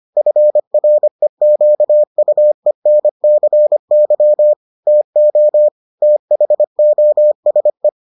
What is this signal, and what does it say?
Morse code, heard as a keyed tone, about 25 wpm, 600 Hz. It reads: FREQUENCY TO THOSE